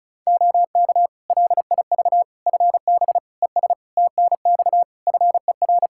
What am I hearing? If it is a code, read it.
Morse code, heard as a keyed tone, about 36 wpm, 700 Hz. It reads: OK LIV FB ES TNX FER
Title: Practice QSO - Exchange 2 - Info - 35wpm